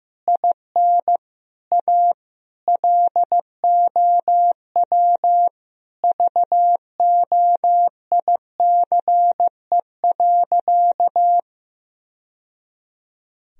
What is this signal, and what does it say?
Morse code, heard as a keyed tone, about 16 wpm, 700 Hz. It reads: IN A LOW VOICE.
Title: Sentences from Top 1000 Words - Review Part 1 - 15wpm